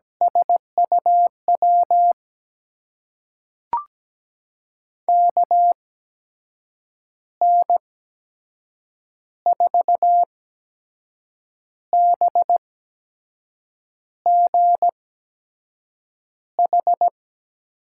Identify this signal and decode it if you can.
Morse code, heard as a keyed tone, about 17 wpm, spaced as for 6 wpm, 700 Hz. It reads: SUW KN4BGH
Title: USA Call Signs - 6 Characters - 8x Character Spacing - 17wpm